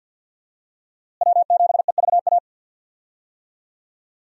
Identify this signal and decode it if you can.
Morse code, heard as a keyed tone, about 51 wpm, 700 Hz. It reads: W7EVA
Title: USA Call Signs - 5 Characters - Repeated 3x - 50wpm